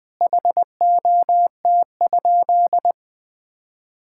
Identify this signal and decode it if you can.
Morse code, heard as a keyed tone, about 20 wpm, 700 Hz. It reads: HOT?
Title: Sentences from Top 1000 Words - Review Part 6 - 20wpm